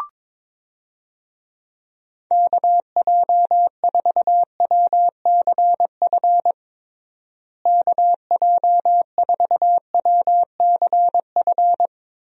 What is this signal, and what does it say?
Morse code, heard as a keyed tone, about 22 wpm, 700 Hz. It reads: KJ4WCF KJ4WCF
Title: USA Call Signs - 6 Characters - Repeated 3x - 22wpm